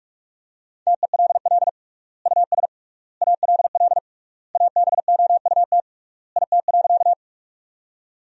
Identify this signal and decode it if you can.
Morse code, heard as a keyed tone, about 45 wpm, 700 Hz. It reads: TELL US ALL ABOUT IT.
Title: 3x Word-Spacing - Sentences from Top 200 Words - 45wpm